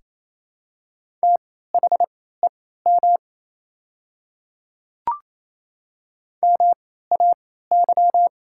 Morse code, heard as a keyed tone, about 28 wpm, spaced as for 16 wpm, 700 Hz. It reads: THEM MAY
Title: 3x Character Spacing - Top 100 Words - 28wpm